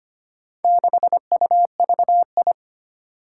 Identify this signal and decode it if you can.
Morse code, heard as a keyed tone, about 25 wpm, 700 Hz. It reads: 6UVI
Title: Four Challenging Letter-Number Combinations 25wpm